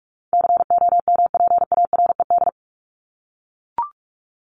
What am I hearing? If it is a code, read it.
Morse code, heard as a keyed tone, about 44 wpm, 700 Hz. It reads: COMPARED
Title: ED Suffix - Top 100 Words with Base Words 45wpm